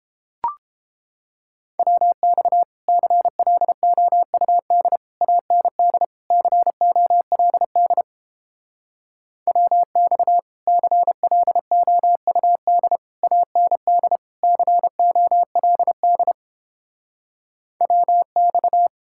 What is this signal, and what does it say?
Morse code, heard as a keyed tone, about 29 wpm, 700 Hz. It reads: WX CLOUD AND COLD WX CLOUD AND COLD WX